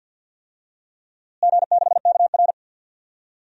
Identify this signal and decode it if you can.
Morse code, heard as a keyed tone, about 52 wpm, 700 Hz. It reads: G6KR
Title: DXCC-100 Call Signs - 4 Characters - Repeated 2x - 50wpm